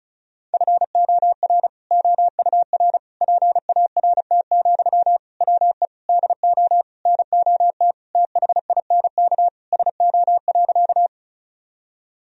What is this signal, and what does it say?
Morse code, heard as a keyed tone, about 34 wpm, 700 Hz. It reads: FOR OUR PART, WE DO NOT THINK SO.
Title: Sentences from Top 1000 Words - Review Part 1 - 35wpm